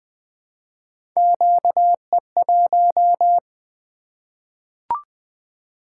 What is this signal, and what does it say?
Morse code, heard as a keyed tone, about 20 wpm, 700 Hz. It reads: QE1
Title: Three Letter-Number Combinations - 20wpm